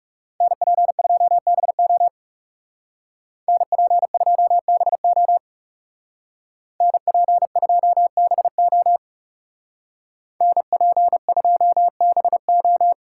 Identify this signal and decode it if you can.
Morse code, heard as a keyed tone, about 35 wpm, 700 Hz. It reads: NP2BO NP2BO NP2BO NP2BO
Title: Speed Racing - USA 2x2 Call Signs 30wpm